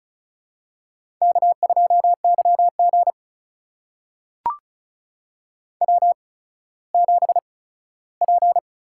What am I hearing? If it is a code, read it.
Morse code, heard as a keyed tone, about 36 wpm, spaced as for 17 wpm, 700 Hz. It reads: K2YG W7P
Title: USA Call Signs - 4 Characters - 8x Character Spacing - 35wpm